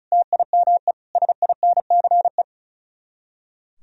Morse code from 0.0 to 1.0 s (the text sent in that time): TIME